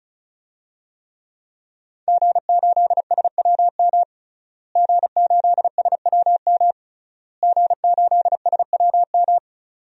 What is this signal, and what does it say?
Morse code, heard as a keyed tone, about 34 wpm, 700 Hz. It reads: G8SWM G8SWM G8SWM
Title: DXCC-100 Call Signs - 5 Characters - Repeated 3x - 35wpm